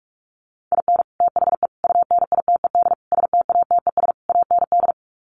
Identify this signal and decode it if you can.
Morse code, heard as a keyed tone, about 44 wpm, 700 Hz. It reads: IN THE UNITED STATES AND
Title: Sets of 5 Words 45wpm